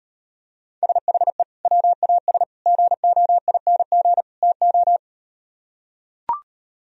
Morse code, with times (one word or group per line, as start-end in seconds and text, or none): 0.8-1.4 SHE
1.7-2.4 WAS
2.7-4.2 GOING
4.4-5.0 TO